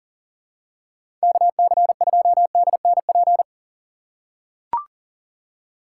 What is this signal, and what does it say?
Morse code, heard as a keyed tone, about 40 wpm, 700 Hz. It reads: KC2DNP
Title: USA Call Signs - 6 Characters - 4x Character Spacing - 40wpm